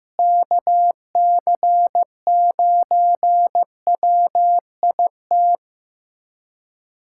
Morse code, heard as a keyed tone, about 16 wpm, 700 Hz. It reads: KC9WIT
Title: USA Callsign DE Callsign 15wpm